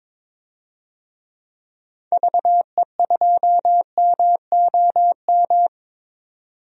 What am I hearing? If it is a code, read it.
Morse code, heard as a keyed tone, about 22 wpm, 700 Hz. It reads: VE2MOM
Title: DXCC-100 Call Signs - 6 Characters - 22wpm